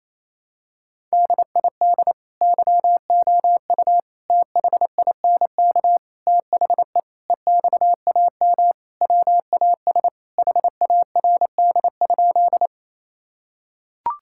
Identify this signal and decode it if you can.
Morse code, heard as a keyed tone, about 28 wpm, 700 Hz. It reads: DID YOU THINK THE EXAM WAS HARD?
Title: Sentences from Top 1000 Words - Review Part 5 - 28wpm